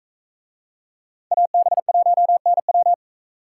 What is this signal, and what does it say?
Morse code, heard as a keyed tone, about 42 wpm, 700 Hz. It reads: AB1NW